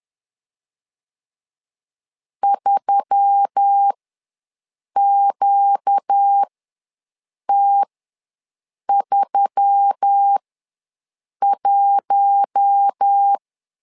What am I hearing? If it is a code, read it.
Morse code, heard as a keyed tone, about 10 wpm, spaced as for 6 wpm, 800 Hz. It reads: 3QT31
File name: LEZIONI / 4 Q1S3 ws 10wpm 2437